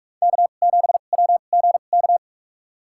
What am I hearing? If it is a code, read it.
Morse code, heard as a keyed tone, about 45 wpm, spaced as for 32 wpm, 700 Hz. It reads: K7WGK
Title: USA Call Signs - 5 Characters - 2x Character Spacing - 45wpm